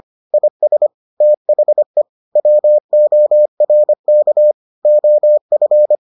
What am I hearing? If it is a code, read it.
Morse code, heard as a keyed tone, about 25 wpm, 600 Hz. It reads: IS THE WORK OF